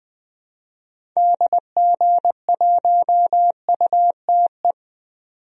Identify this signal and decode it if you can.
Morse code, heard as a keyed tone, about 21 wpm, 700 Hz. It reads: DG1UTE
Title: DXCC-100 Call Signs - 6 Characters - 20wpm